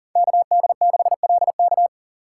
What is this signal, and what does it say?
Morse code, heard as a keyed tone, about 40 wpm, 700 Hz. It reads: KD6LK
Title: USA Call Signs - 5 Characters - 3x Character Spacing - 40wpm